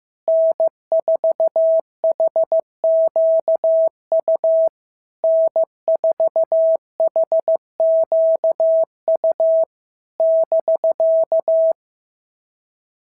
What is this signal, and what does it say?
Morse code, heard as a keyed tone, about 15 wpm, 650 Hz. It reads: N4HQU N4HQU <BK>
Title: SOTA QSO Practice 15wpm